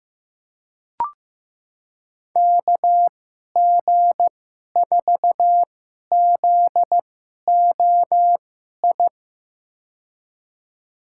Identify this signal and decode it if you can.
Morse code, heard as a keyed tone, about 15 wpm, spaced as for 11 wpm, 700 Hz. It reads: KG4ZOI
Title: USA Call Signs - 6 Characters - 2x Character Spacing - 15wpm